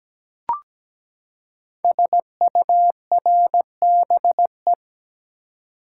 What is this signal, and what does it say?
Morse code, heard as a keyed tone, about 17 wpm, 700 Hz. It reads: SURBE
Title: Code Groups - 5 Letters - 17wpm